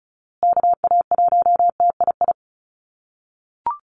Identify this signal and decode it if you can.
Morse code, heard as a keyed tone, about 34 wpm, 700 Hz. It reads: KA1TII
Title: USA Call Signs Repeated 3x 35wpm